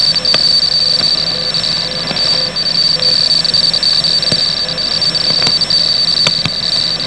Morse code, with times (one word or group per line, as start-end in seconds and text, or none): none